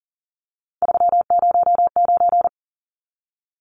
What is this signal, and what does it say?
Morse code, heard as a keyed tone, about 37 wpm, 700 Hz. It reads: 309
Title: Numbers - 3-Digit 40wpm